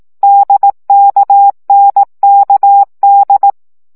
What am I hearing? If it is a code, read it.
Morse code, heard as a keyed tone, about 18 wpm, 800 Hz. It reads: DKNKD
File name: CW gyakorlas n d k 15 baud